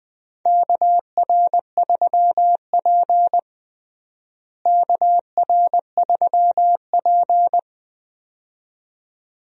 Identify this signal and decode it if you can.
Morse code, heard as a keyed tone, about 20 wpm, 700 Hz. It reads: KR3P KR3P
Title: USA Call Signs - 4 Characters - Repeated 2x - 20wpm